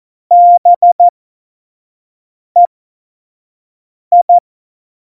Morse code, heard as a keyed tone, about 14 wpm, spaced as for 5 wpm, 700 Hz. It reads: BEI
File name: Morsekurs / Witze / Witz 28. Januar 2020, 25 F70